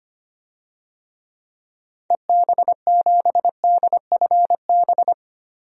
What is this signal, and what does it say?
Morse code, heard as a keyed tone, about 25 wpm, 700 Hz. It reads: EB7DFB
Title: DXCC-100 Call Signs - 6 Characters - 25wpm